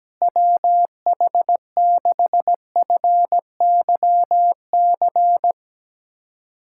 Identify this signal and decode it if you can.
Morse code, heard as a keyed tone, about 17 wpm, 700 Hz. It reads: WH6FYC
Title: USA Callsign DE Callsign 17wpm